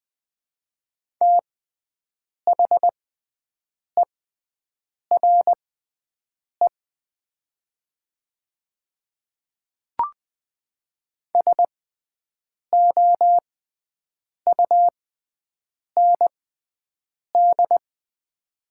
Morse code, heard as a keyed tone, about 20 wpm, spaced as for 7 wpm, 700 Hz. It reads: THERE SOUND
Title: 6x Character Spacing - Top 100 Words - 20wpm